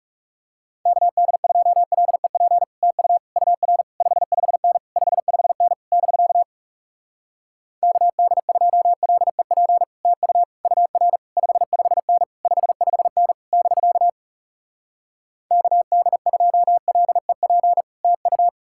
Morse code, heard as a keyed tone, about 40 wpm, 700 Hz. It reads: KD2LEP TU UR 55N 55N <BK> KD2LEP TU UR 55N 55N <BK> KD2LEP TU